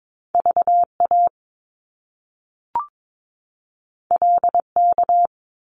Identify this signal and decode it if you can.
Morse code, heard as a keyed tone, about 22 wpm, 700 Hz. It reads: VA LK
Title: Top Two Letter Combinations in Words 22wpm